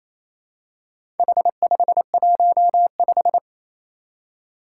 Morse code, heard as a keyed tone, about 27 wpm, 700 Hz. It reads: H515
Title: Four Challenging Letter-Number Combinations 28wpm